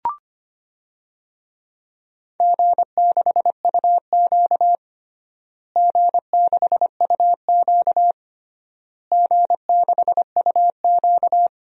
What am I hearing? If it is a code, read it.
Morse code, heard as a keyed tone, about 25 wpm, 700 Hz. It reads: G6UQ G6UQ G6UQ
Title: DXCC-100 Call Signs - 4 Characters - Repeated 3x - 25wpm